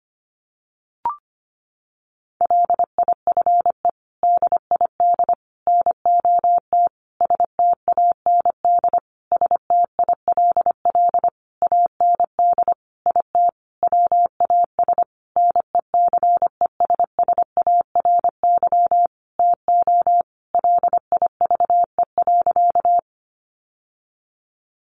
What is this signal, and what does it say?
Morse code, heard as a keyed tone, about 25 wpm, 700 Hz. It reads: LIFE DID NOT STAND STILL AND IT WAS NECESSARY TO LIVE.
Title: Sentences from Top 800 Words 25wpm